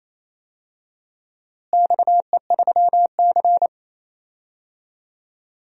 Morse code, heard as a keyed tone, about 28 wpm, 700 Hz. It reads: XE3C
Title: DXCC-100 Call Signs - 4 Characters - Repeated 2x - 28wpm